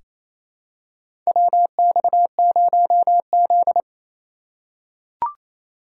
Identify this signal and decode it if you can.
Morse code, heard as a keyed tone, about 28 wpm, 700 Hz. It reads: WX0Z
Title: K1USN SST QSO Practice - 28wpm